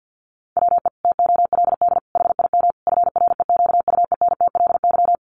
Morse code, heard as a keyed tone, about 50 wpm, 700 Hz. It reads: WE TOLD HIM FREQUENTLY
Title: Sentences from Top 1300 Words 50wpm